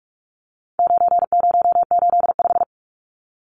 Morse code, heard as a keyed tone, about 44 wpm, 700 Hz. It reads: 9085
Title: Numbers - 4-Digit 45wpm